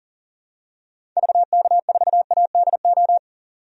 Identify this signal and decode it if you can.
Morse code, heard as a keyed tone, about 40 wpm, 700 Hz. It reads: VK4ADO
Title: DXCC-100 Call Signs - 6 Characters - 40wpm